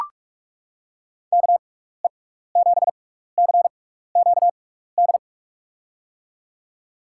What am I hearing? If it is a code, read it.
Morse code, heard as a keyed tone, about 44 wpm, spaced as for 15 wpm, 700 Hz. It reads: KE7CQD
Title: USA Call Signs - 6x Character Spacing - 45wpm